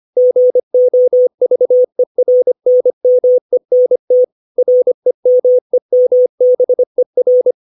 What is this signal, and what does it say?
Morse code, heard as a keyed tone, about 25 wpm, 500 Hz. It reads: GOVERNMENT REMEMBER
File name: 632 words / 632 Words 25WPM